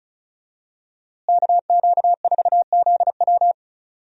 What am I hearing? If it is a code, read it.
Morse code, heard as a keyed tone, about 34 wpm, 700 Hz. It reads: KQ4ZW
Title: USA Call Signs - 5 Characters - Repeated 2x - 35wpm